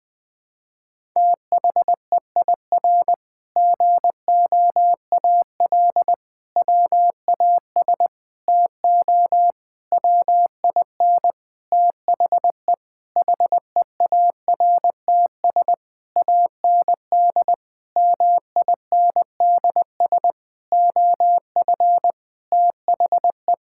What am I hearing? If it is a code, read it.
Morse code, heard as a keyed tone, about 20 wpm, 700 Hz. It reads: THEIR GOAL WAS TO WIN THE HEARTS AND MINDS OF THE
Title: Sentences with 8-Word Phrases - 20wpm